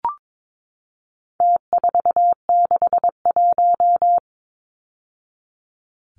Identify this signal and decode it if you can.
Morse code, heard as a keyed tone, about 22 wpm, 700 Hz. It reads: T461
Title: Numbers - Contest Style - 4-Digit 22wpm